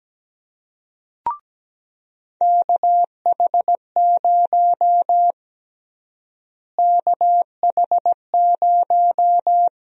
Morse code, heard as a keyed tone, about 17 wpm, 700 Hz. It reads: KH0 KH0
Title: USA Call Sign Prefixes - Repeated 2x - 17wpm